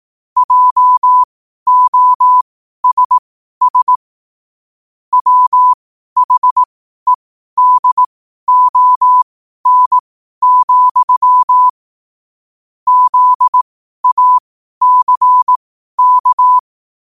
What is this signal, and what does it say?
Morse code, heard as a keyed tone, about 18 wpm, spaced as for 12 wpm, 1000 Hz. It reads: JOSS WHEDON, ZACK
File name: qotdfiles / 12wpm